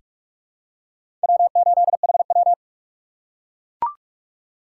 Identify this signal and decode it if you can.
Morse code, heard as a keyed tone, about 43 wpm, 700 Hz. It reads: W8HW
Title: K1USN SST QSO Practice - 45wpm